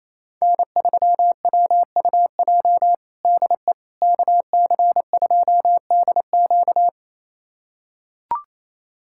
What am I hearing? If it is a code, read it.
Morse code, heard as a keyed tone, about 28 wpm, 700 Hz. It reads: N3WUJ DE KC2DQ
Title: USA Callsign DE Callsign - 5 Characters - 28wpm